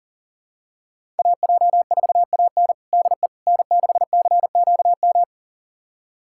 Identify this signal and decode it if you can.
Morse code, heard as a keyed tone, about 37 wpm, 700 Hz. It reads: AJ4AN DE N6CQM
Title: USA Callsign DE Callsign - 5 Characters - 40wpm